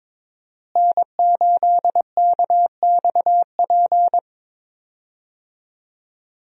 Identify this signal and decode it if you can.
Morse code, heard as a keyed tone, about 22 wpm, 700 Hz. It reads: N8KXP
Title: USA Call Signs - 5 Characters - Repeated 3x - 22wpm